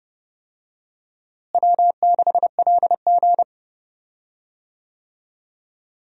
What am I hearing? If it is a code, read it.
Morse code, heard as a keyed tone, about 32 wpm, 700 Hz. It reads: W6LG
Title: USA Call Signs - 4 Characters - Repeated 3x - 30wpm